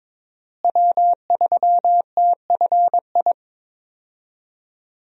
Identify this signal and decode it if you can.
Morse code, heard as a keyed tone, about 22 wpm, 700 Hz. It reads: W3TFI
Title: USA Call Signs - 5 Characters - Repeated 3x - 22wpm